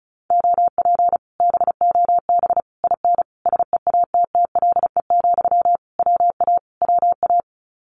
Morse code, heard as a keyed tone, about 34 wpm, 700 Hz. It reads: OP BOB IN SEATTLE, WA WA
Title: Common QSO Phrases Encore 35wpm